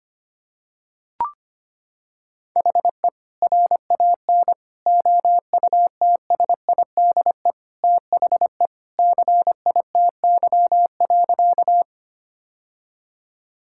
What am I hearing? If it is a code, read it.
Morse code, heard as a keyed tone, about 25 wpm, 700 Hz. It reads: HE RAN OUTSIDE THE CITY.
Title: Sentences from Top 1000 Words - Review Part 6 - 25wpm